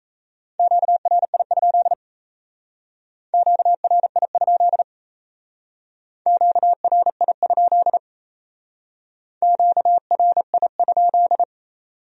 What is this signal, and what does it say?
Morse code, heard as a keyed tone, about 36 wpm, 700 Hz. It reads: QRI? QRI? QRI? QRI?